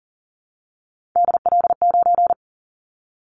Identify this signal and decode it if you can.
Morse code, heard as a keyed tone, about 40 wpm, 700 Hz. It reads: DL9
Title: Callsign Prefixes for Top 50 DXCC - 40wpm